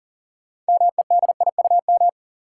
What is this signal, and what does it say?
Morse code, heard as a keyed tone, about 40 wpm, 700 Hz. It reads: MEDIUM
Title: Sentences from Top 900 Words - New Word Drill - 40wpm